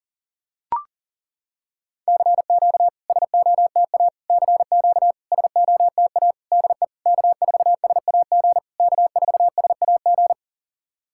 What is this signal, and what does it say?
Morse code, heard as a keyed tone, about 38 wpm, 690 Hz. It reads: CQ SOTA CQ SOTA DE K4SAG K4SAG
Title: SOTA QSO Practice 40wpm